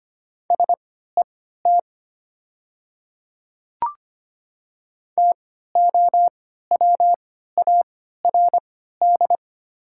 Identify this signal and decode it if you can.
Morse code, heard as a keyed tone, about 25 wpm, spaced as for 14 wpm, 700 Hz. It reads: SET TOWARD
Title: 3x Character Spacing - Top 500 Words - 25wpm